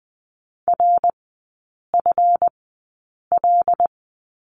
Morse code, heard as a keyed tone, about 20 wpm, spaced as for 8 wpm, 700 Hz. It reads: RFL
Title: 152 Lesson 25e - Letter p - Three Characters 20wpm